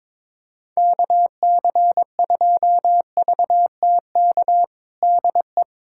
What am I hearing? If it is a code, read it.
Morse code, heard as a keyed tone, about 21 wpm, 700 Hz. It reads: KC2VTK DE
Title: USA Callsign DE Callsign 22wpm